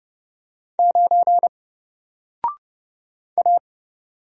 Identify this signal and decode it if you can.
Morse code, heard as a keyed tone, about 31 wpm, 700 Hz. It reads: 9 A
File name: Single-Letter-Number-30wpm